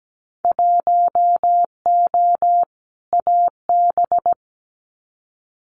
Morse code, heard as a keyed tone, about 17 wpm, 700 Hz. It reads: 1O AB
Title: Winter Field Day Exchanges Encore 17wpm